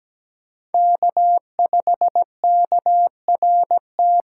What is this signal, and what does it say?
Morse code, heard as a keyed tone, about 20 wpm, 700 Hz. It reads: K5KRT
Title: Speed Racing - USA 1x3 Call Signs 17wpm